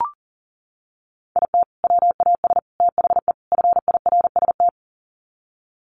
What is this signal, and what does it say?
Morse code, heard as a keyed tone, about 40 wpm, 700 Hz. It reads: IT WAS THE FIRST
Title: Sets of 4 Words 40wpm